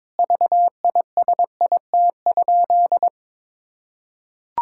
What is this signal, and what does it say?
Morse code, heard as a keyed tone, about 22 wpm, 700 Hz. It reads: VISIT?
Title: Sentences from Top 1000 Words - Review Part 6 - 22wpm